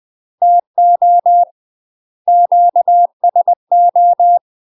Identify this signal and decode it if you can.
Morse code, heard as a keyed tone, about 20 wpm, 700 Hz. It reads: TO QSO
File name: UK MASTER QSOs TRAINING / 20WPM / UK MASTER QSOs 0064